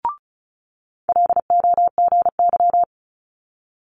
Morse code, heard as a keyed tone, about 33 wpm, 700 Hz. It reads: LOGY
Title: Multi-letter Phonemes Quick Fire 35wpm